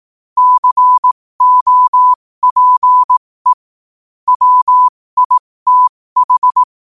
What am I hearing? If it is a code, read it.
Morse code, heard as a keyed tone, about 18 wpm, spaced as for 15 wpm, 1000 Hz. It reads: COPE WITH
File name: qotdfiles / 15wpm